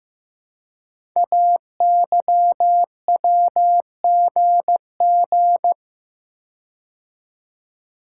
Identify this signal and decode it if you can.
Morse code, heard as a keyed tone, about 15 wpm, 700 Hz. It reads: AYWGG
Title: Code Groups - 5 Letters - 15wpm